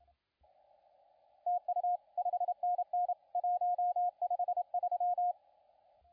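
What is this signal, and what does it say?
Morse code, heard as a keyed tone, about 31 wpm, 710 Hz, with radio noise. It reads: TU 5NN 153
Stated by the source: CW contest audio recorded at LA4C during the SAC CW contest.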